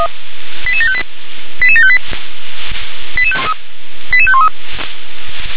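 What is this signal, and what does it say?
Five-tone selective call (ZVEI), heard as quick runs of five tones, tones of about 70 ms. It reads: 88758 88758 88513
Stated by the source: Received on an SDR; ZVEI five-tone call.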